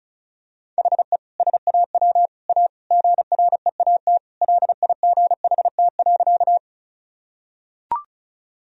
Morse code, heard as a keyed tone, about 36 wpm, 700 Hz. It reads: HE SAW A GREAT LIGHT.
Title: Sentences from Top 1000 Words - Review Part 1 - 35wpm